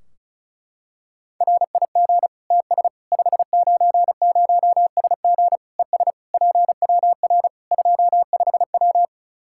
Morse code, heard as a keyed tone, about 34 wpm, 700 Hz. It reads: RIG TS 590SG ES PWR 25W
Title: K1USN SST QSO Practice 35wpm